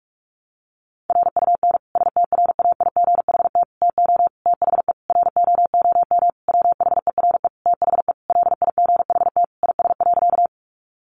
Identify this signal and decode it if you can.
Morse code, heard as a keyed tone, about 46 wpm, 700 Hz. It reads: RUN STRAIGHT TO THE ROOM WHERE THE LIGHT IS.
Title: Sentences from Top 1000 Words 45wpm